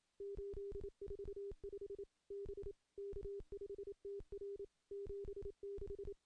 Morse code, heard as a keyed tone, about 28 wpm, 400 Hz. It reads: 845 B K5TR 76
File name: contests / CW-SS-k5tr- -1300